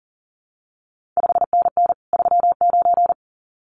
Morse code, heard as a keyed tone, about 42 wpm, 700 Hz. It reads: 5NN 39
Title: CQ WW DX Contest Exchange Encore - 40wpm